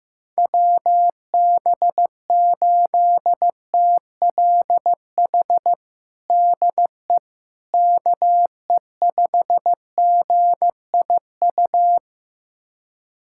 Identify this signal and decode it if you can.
Morse code, heard as a keyed tone, about 16 wpm, 700 Hz. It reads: WB8TLH DE KE5GIU
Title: USA Callsign DE Callsign - 6 Characters - 15wpm